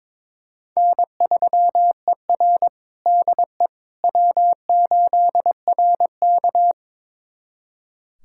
Morse code, heard as a keyed tone, about 22 wpm, 700 Hz. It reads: N3ER DE W8RK
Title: USA Callsign DE Callsign - 4 Characters - 22wpm